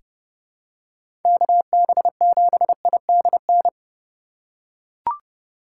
Morse code, heard as a keyed tone, about 30 wpm, 700 Hz. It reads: KB7IDN
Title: USA Call Signs - 6 Characters - 4x Character Spacing - 30wpm